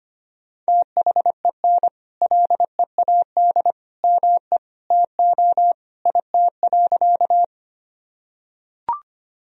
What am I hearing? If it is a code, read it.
Morse code, heard as a keyed tone, about 25 wpm, 700 Hz. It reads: THEN LEAD ME TO IT.
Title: Sentences from Top 1000 Words - Review Part 2 - 25wpm